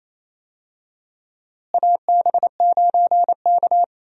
Morse code, heard as a keyed tone, about 28 wpm, 700 Hz. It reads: AB9K
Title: USA Call Signs - 4 Characters - Repeated 2x - 28wpm